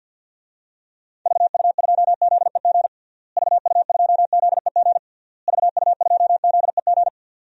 Morse code, heard as a keyed tone, about 53 wpm, 700 Hz. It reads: VU2ZEG VU2ZEG VU2ZEG
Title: DXCC-100 Call Signs - 6 Characters - Repeated 3x - 50wpm